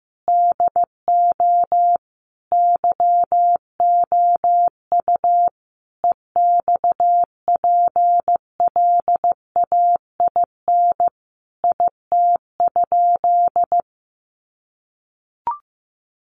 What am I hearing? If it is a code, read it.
Morse code, heard as a keyed tone, about 15 wpm, 700 Hz. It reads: DO YOU EXPLAIN IT?
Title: Sentences from Top 1100 Words 15wpm